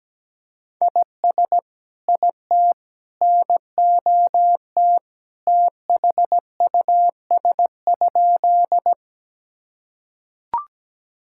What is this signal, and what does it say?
Morse code, heard as a keyed tone, about 17 wpm, 700 Hz. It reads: IS IT NOT THUS?
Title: Sentences from Top 1000 Words - Review Part 5 - 17wpm